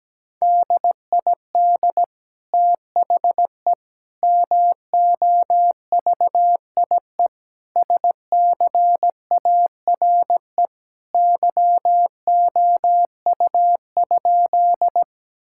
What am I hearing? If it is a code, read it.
Morse code, heard as a keyed tone, about 17 wpm, 700 Hz. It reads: DID THE MOVIE SCARE YOU?
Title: Sentences from Top 1000 Words - Review Part 5 - 17wpm